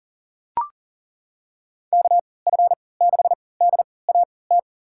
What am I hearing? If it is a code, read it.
Morse code, heard as a keyed tone, about 40 wpm, spaced as for 23 wpm, 700 Hz. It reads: KF6DAT
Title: USA Call Signs - 6 Characters - 3x Character Spacing - 40wpm